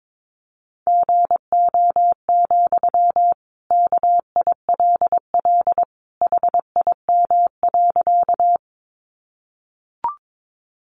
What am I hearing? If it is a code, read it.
Morse code, heard as a keyed tone, about 22 wpm, 700 Hz. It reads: GO, KILL HIM.
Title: Sentences from Top 1400 Words 22wpm